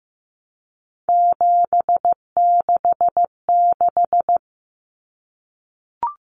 Morse code, heard as a keyed tone, about 15 wpm, 700 Hz. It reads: 766
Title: Numbers - 3-Digit 15wpm